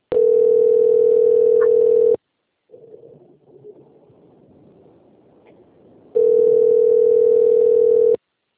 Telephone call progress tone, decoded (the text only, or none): ringback tone